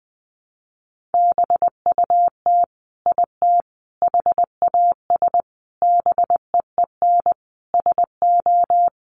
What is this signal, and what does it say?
Morse code, heard as a keyed tone, about 20 wpm, 700 Hz. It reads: BUT IT HAS BEEN SO
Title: Sentences from Top 100 Words 20wpm